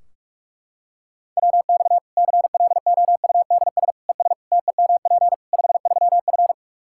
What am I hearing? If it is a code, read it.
Morse code, heard as a keyed tone, about 46 wpm, 700 Hz. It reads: WX CLOUDS ES TEMP 53F
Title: Practice QSO - Exchange 2 - Info - 45wpm